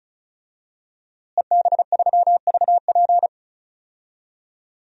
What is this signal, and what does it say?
Morse code, heard as a keyed tone, about 36 wpm, 700 Hz. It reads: EB3VP
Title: DXCC-100 Call Signs - 5 Characters - Repeated 3x - 35wpm